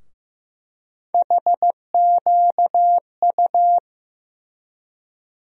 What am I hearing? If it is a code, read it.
Morse code, heard as a keyed tone, about 15 wpm, 700 Hz. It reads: HQU
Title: USA Call Sign Postfixes - 3 Letters 15wpm